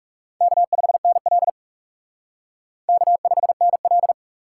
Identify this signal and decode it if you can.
Morse code, heard as a keyed tone, about 37 wpm, 700 Hz. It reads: K5NL K5NL
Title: Speed Racing - USA 1x2 Call Signs 30wpm